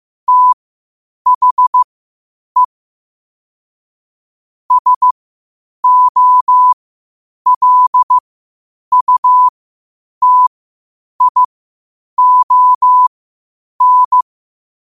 Morse code, heard as a keyed tone, about 15 wpm, spaced as for 8 wpm, 1000 Hz. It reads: THE SOLUTION
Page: 8 wpm morse code quotes